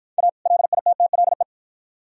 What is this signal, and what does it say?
Morse code, heard as a keyed tone, about 51 wpm, 700 Hz. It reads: A LITTLE